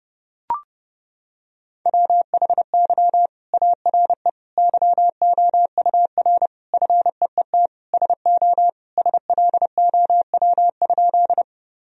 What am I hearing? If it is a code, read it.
Morse code, heard as a keyed tone, about 32 wpm, 700 Hz. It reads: WHY ARE YOUR FEET SO SLOW?
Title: Sentences from Top 1000 Words - Review Part 2 - 30wpm